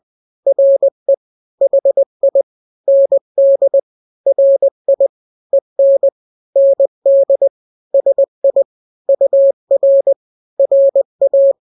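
Morse code, heard as a keyed tone, about 20 wpm, 550 Hz. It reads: RE HI ND RI EN ND SI UR RA
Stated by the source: Morse drill of two-letter n-grams at 20/18 WPM (Farnsworth), from KK7RR.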